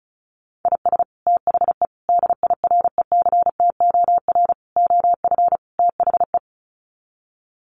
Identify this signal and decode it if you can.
Morse code, heard as a keyed tone, about 34 wpm, 700 Hz. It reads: IS THE DIRECTOR OF THE
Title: Sets of 5 Words Encore2 35wpm